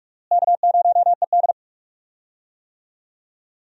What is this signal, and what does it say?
Morse code, heard as a keyed tone, about 43 wpm, 700 Hz. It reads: K0ED
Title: USA Call Signs - 4 Characters - Repeated 3x - 45wpm